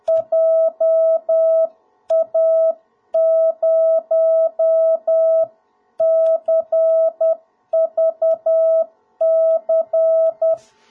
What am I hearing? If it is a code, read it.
Morse code, heard as a keyed tone, about 10 wpm, 650 Hz. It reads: JA0CVC
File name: jh8cbh / 2021cw / cw01b